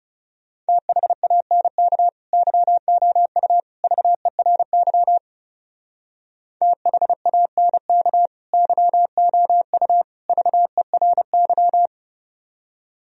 Morse code, heard as a keyed tone, about 35 wpm, 700 Hz. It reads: THANK YOU VERY THANK YOU VERY
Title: Speed-Racing - Sets of 3 Words 30wpm